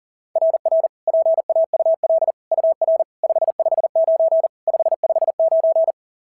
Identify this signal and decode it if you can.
Morse code, heard as a keyed tone, about 37 wpm, 690 Hz. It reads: RR PAUL UR 559 559
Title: SOTA QSO Practice 40wpm